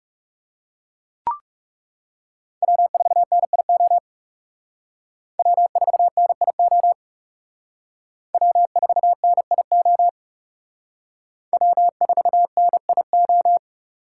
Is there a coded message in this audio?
Yes: Morse code from 2.6 to 13.6 s, heard as a keyed tone, about 39 wpm, 700 Hz. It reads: W4NIO W4NIO W4NIO W4NIO